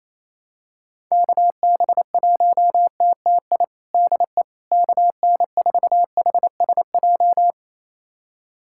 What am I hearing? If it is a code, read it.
Morse code, heard as a keyed tone, about 28 wpm, 700 Hz. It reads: KB1TTI DE KN4HSJ
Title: USA Callsign DE Callsign 28wpm